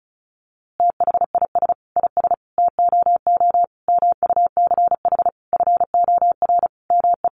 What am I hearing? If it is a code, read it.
Morse code, heard as a keyed tone, about 34 wpm, 700 Hz. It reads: THIS IS TOO MUCH FOR ME
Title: A Sound Of Thunder 35wpm